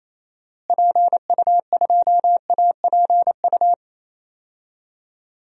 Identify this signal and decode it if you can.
Morse code, heard as a keyed tone, about 28 wpm, 700 Hz. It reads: PU2APU
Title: DXCC-100 Call Signs - 6 Characters - Repeated 2x - 28wpm